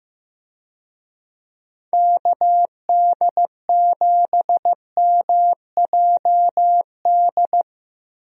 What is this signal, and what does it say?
Morse code, heard as a keyed tone, about 15 wpm, 700 Hz. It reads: KD7MJD
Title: USA Call Signs - 6 Characters - Repeated 3x - 15wpm